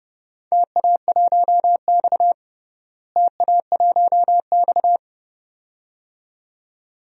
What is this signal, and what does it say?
Morse code, heard as a keyed tone, about 30 wpm, 700 Hz. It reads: TA1X TA1X
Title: DXCC-100 Call Signs - 4 Characters - Repeated 2x - 30wpm